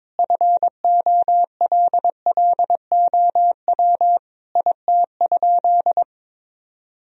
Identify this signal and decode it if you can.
Morse code, heard as a keyed tone, about 22 wpm, 700 Hz. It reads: FOLLOW IT?
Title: Sentences from Top 1000 Words - Review Part 1 - 22wpm